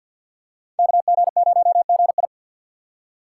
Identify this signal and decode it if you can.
Morse code, heard as a keyed tone, about 47 wpm, 700 Hz. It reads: KG0GI
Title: USA Call Signs - 5 Characters - 2x Character Spacing - 50wpm